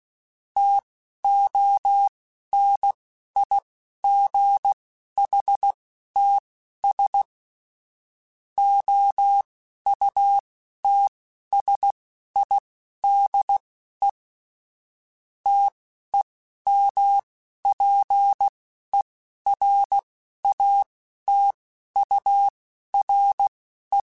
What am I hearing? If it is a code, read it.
Morse code, heard as a keyed tone, about 16 wpm, spaced as for 11 wpm, 800 Hz. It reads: TONIGHTS OUTSIDE TEMPERATURE
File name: Spaced Characters / Spaced Characters-mp3s / Weather 4 14 sp @15